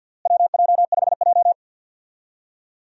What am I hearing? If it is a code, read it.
Morse code, heard as a keyed tone, about 46 wpm, 700 Hz. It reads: WJ5J
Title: USA Call Signs - 4 Characters - Repeated 2x - 50wpm